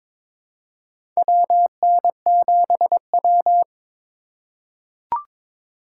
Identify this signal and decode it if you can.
Morse code, heard as a keyed tone, about 22 wpm, 700 Hz. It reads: WN7W
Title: USA Call Signs - 4 Characters - 2x Character Spacing - 22wpm